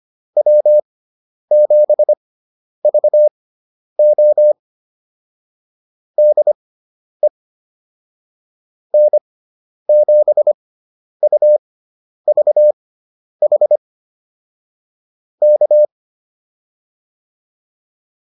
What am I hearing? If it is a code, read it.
Morse code, heard as a keyed tone, about 25 wpm, spaced as for 10 wpm, 600 Hz. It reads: W7VO DE N7UVH K